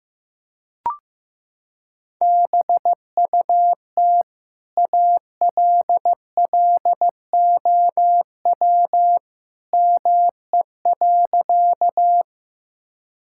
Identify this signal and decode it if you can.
Morse code, heard as a keyed tone, about 15 wpm, 700 Hz. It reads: BUT ALLOW ME.
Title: Sentences from Top 1000 Words - Review Part 5 - 15wpm